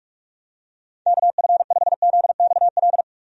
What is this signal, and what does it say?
Morse code, heard as a keyed tone, about 46 wpm, 700 Hz. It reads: KF5ZXL
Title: USA Call Signs - 6 Characters - 45wpm